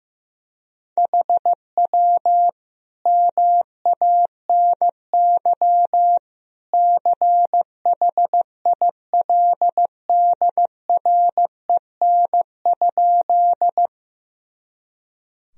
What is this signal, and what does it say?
Morse code, heard as a keyed tone, about 15 wpm, 700 Hz. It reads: HW MANY CHILDREN?
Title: Sentences from Top 300 Words and Abbreviations 15wpm